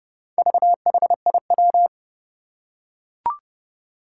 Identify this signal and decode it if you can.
Morse code, heard as a keyed tone, about 30 wpm, 700 Hz. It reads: VHIW
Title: Four Challenging Letter-Number Combinations 30wpm